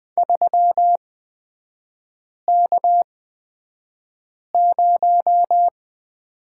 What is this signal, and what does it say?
Morse code, heard as a keyed tone, about 20 wpm, spaced as for 5 wpm, 700 Hz. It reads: 3K0
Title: 229 Lesson 37e - Letter x - Three Characters 20wpm